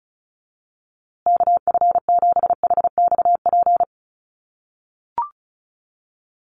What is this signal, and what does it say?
Morse code, heard as a keyed tone, about 34 wpm, 700 Hz. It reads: KF7HXP
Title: USA Call Signs Repeated 3x 35wpm